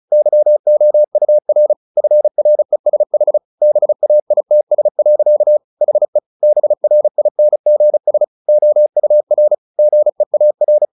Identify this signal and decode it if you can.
Morse code, heard as a keyed tone, about 34 wpm, 600 Hz. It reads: YOUR FRESH BAITS. HE BRINGS OUR GEAR
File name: the old man and the sea / the old man and the sea 1 35wpm